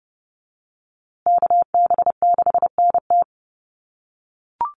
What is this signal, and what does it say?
Morse code, heard as a keyed tone, about 30 wpm, 700 Hz. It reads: KB6NT
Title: USA Call Signs Repeated 3x 30wpm